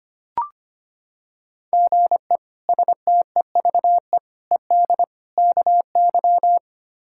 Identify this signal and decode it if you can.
Morse code, heard as a keyed tone, about 25 wpm, 700 Hz. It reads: GE STEVE ED KY
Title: K1USN SST QSO Practice - 25wpm